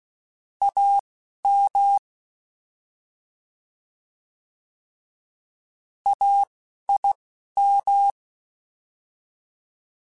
Morse code, heard as a keyed tone, about 16 wpm, 800 Hz. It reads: A M A I M